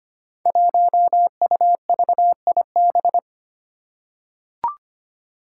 Morse code, heard as a keyed tone, about 25 wpm, 700 Hz. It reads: 1UVIB
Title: Five Challenging Letter-Number Combinations 25wpm